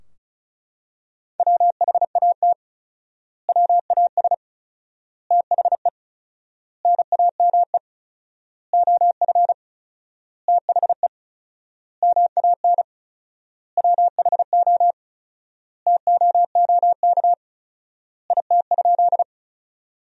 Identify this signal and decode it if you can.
Morse code, heard as a keyed tone, about 36 wpm, 700 Hz. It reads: WHAT WAS THE NAME OF THE MAN WHO TOOK IT?
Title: 4x Word-Spacing - Sentences from Top 300 Words - 35wpm